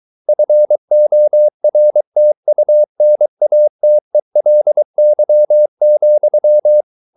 Morse code, heard as a keyed tone, about 23 wpm, 600 Hz. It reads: FORTUNATELY,